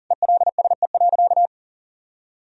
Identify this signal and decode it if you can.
Morse code, heard as a keyed tone, about 40 wpm, 700 Hz. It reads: ELSE.
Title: Sentences from Top 1000 Words - Review Part 1 - 40wpm